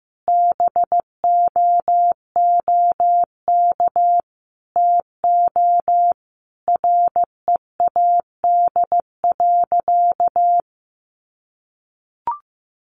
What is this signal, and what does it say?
Morse code, heard as a keyed tone, about 15 wpm, 700 Hz. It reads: BOOK TO READ.
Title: Sentences from Top 1300 Words 15wpm